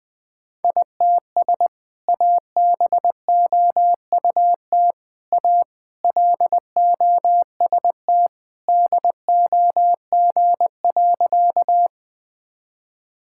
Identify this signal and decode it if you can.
Morse code, heard as a keyed tone, about 20 wpm, 700 Hz. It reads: ITS ABOUT A LOST DOG.
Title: Sentences with 2-Word Phrases - 20wpm